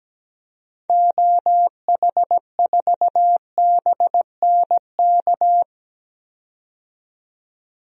Morse code, heard as a keyed tone, about 17 wpm, 700 Hz. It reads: OH4BNK
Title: DXCC-100 Call Signs - 6 Characters - 17wpm